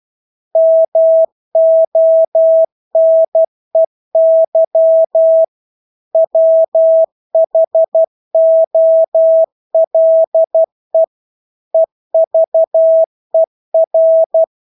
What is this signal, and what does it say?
Morse code, heard as a keyed tone, about 12 wpm, 650 Hz. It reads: MONEY WHOLE EVER
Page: Engelske ord 12 wpm